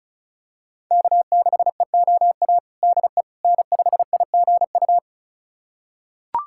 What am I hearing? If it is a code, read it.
Morse code, heard as a keyed tone, about 36 wpm, 700 Hz. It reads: K6EOA DE N5IGU
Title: USA Callsign DE Callsign - 5 Characters - 35wpm